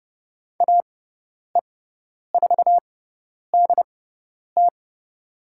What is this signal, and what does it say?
Morse code, heard as a keyed tone, about 31 wpm, spaced as for 10 wpm, 700 Hz. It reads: AE4DT
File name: Lesson 16g - Letter c - Call Signs 30wpm